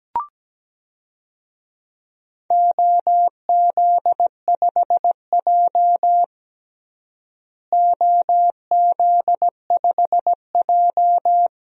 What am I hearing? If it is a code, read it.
Morse code, heard as a keyed tone, about 17 wpm, 700 Hz. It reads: OZ5J OZ5J
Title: DXCC-100 Call Signs - 4 Characters - Repeated 2x - 17wpm